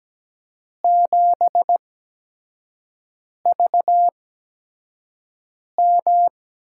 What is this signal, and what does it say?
Morse code, heard as a keyed tone, about 17 wpm, spaced as for 5 wpm, 700 Hz. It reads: 7VM
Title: USA Call Signs - 8x Character Spacing - 17wpm